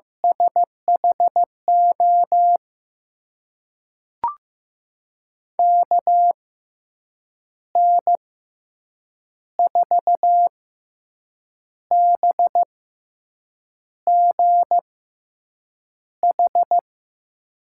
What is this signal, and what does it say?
Morse code, heard as a keyed tone, about 15 wpm, spaced as for 6 wpm, 700 Hz. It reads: SHO KN4BGH
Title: USA Call Signs - 6 Characters - 6x Character Spacing - 15wpm